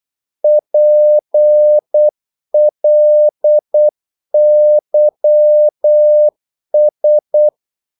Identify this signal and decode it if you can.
Morse code, heard as a keyed tone, about 8 wpm, 600 Hz. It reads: PLYS
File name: Korte ord DK 8wpm